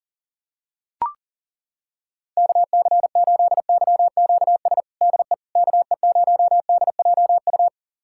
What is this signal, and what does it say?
Morse code, heard as a keyed tone, about 40 wpm, 700 Hz. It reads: KC8YQS DE KE0DJU
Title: USA Callsign DE Callsign 40wpm